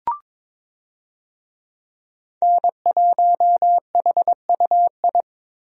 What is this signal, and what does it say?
Morse code, heard as a keyed tone, about 22 wpm, 700 Hz. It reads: N1HUI
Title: USA Call Signs - 5 Characters - Repeated 3x - 22wpm